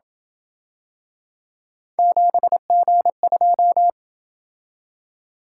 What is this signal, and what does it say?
Morse code, heard as a keyed tone, about 26 wpm, 700 Hz. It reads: 7G2